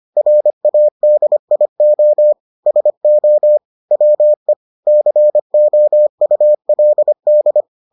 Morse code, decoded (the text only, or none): RADIO SO WE COULD